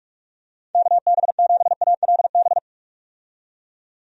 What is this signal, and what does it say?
Morse code, heard as a keyed tone, about 46 wpm, 700 Hz. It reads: KB7ALB
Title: USA Call Signs - 6 Characters - 45wpm